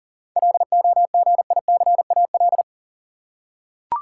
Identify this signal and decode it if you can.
Morse code, heard as a keyed tone, about 40 wpm, 700 Hz. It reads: LOGICAL
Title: Seven Letter Words - 40wpm